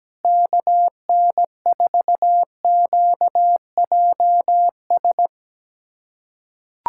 Morse code, heard as a keyed tone, about 17 wpm, 700 Hz. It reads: KN4QJS
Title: USA Callsign DE Callsign 17wpm